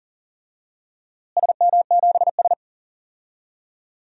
Morse code, heard as a keyed tone, about 39 wpm, 700 Hz. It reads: SM7S
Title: DXCC-100 Call Signs - 4 Characters - Repeated 2x - 40wpm